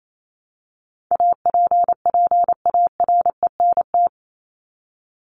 Morse code, heard as a keyed tone, about 28 wpm, 700 Hz. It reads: APPARENT
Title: LY Suffix - Top 100 Words with Base Words 28wpm